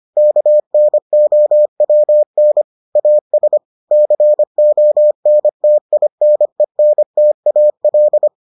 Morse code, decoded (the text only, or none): KNOWN AS CONTINENTAL